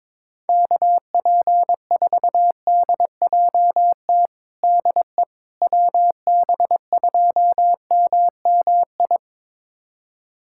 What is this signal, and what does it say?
Morse code, heard as a keyed tone, about 22 wpm, 700 Hz. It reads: KP4DJT DE WB2MMI
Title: USA Callsign DE Callsign - 6 Characters - 22wpm